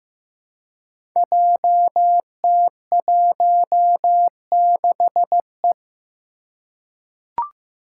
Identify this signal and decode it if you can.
Morse code, heard as a keyed tone, about 15 wpm, 700 Hz. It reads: JT16E
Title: Five Challenging Letter-Number Combinations 15wpm